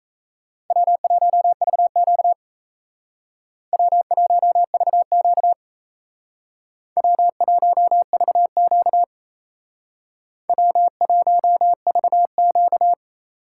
Morse code, heard as a keyed tone, about 32 wpm, 700 Hz. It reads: W1VQ W1VQ W1VQ W1VQ
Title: Speed Racing - USA 1x2 Call Signs 28wpm